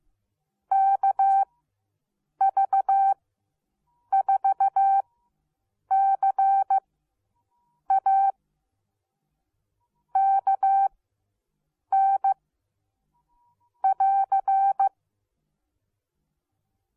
Morse code, heard as a keyed tone, about 15 wpm, spaced as for 7 wpm, 780 Hz. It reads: KV4CA KN <AR>